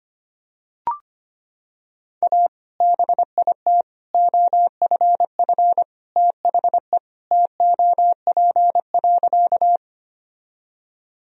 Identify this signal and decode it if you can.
Morse code, heard as a keyed tone, about 25 wpm, 700 Hz. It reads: A BIT OFF THE TOP.
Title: Sentences from Top 1000 Words - Review Part 2 - 25wpm